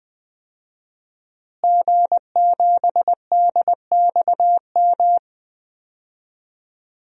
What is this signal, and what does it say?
Morse code, heard as a keyed tone, about 20 wpm, 700 Hz. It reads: G7DXM
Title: DXCC-100 Call Signs - 5 Characters - Repeated 2x - 20wpm